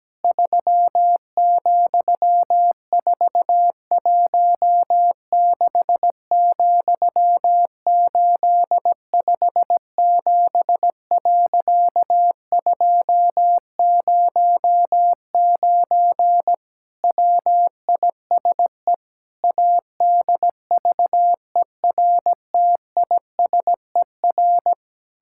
Morse code, heard as a keyed tone, about 17 wpm, 700 Hz. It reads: 3,416,857.209 WISE ADVERTISER
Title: Pangram Challenge v2 - 17wpm